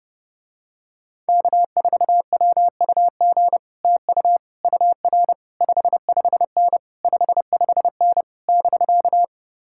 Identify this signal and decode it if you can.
Morse code, heard as a keyed tone, about 30 wpm, 700 Hz. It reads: K4WUG TU UR 55N 55N <BK>